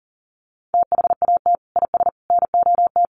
Morse code, heard as a keyed tone, about 40 wpm, 700 Hz. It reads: THAT IS NOT
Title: Sentences from Top 700 Words 40wpm